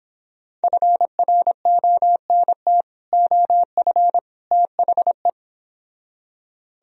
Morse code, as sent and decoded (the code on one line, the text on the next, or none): ..-. .-. --- -. - / --- ..-. / - .... .
FRONT OF THE